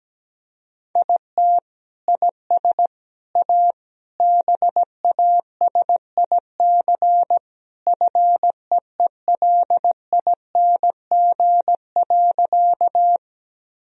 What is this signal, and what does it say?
Morse code, heard as a keyed tone, about 17 wpm, 700 Hz. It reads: IT IS A BASIC FEELING.
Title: Sentences from Top 1000 Words - Review Part 3 - 17wpm